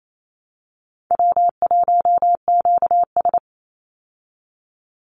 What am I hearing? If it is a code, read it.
Morse code, heard as a keyed tone, about 28 wpm, 700 Hz. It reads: W1QS
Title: Easy Call Signs Repeated 2x - 28wpm